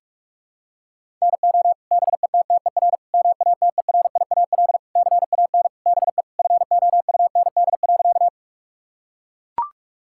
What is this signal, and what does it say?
Morse code, heard as a keyed tone, about 42 wpm, 700 Hz. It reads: NO BETTER MATERIAL CAN BE FOUND.
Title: Sentences from Top 1000 Words - Review Part 5 - 45wpm